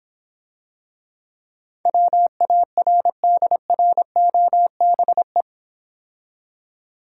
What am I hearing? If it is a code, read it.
Morse code, heard as a keyed tone, about 25 wpm, 700 Hz. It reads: WARDROBE